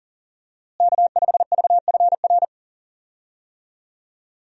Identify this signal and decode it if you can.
Morse code, heard as a keyed tone, about 40 wpm, 700 Hz. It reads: K5VFR
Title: USA Call Signs - 5 Characters - Repeated 2x - 40wpm